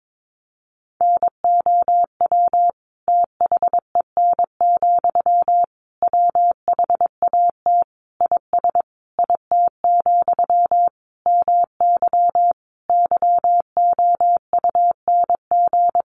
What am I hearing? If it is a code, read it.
Morse code, heard as a keyed tone, about 22 wpm, 700 Hz. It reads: NOW THEN, WHAT IS IT, MY YOUNG
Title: Sentences from Top 1100 Words 22wpm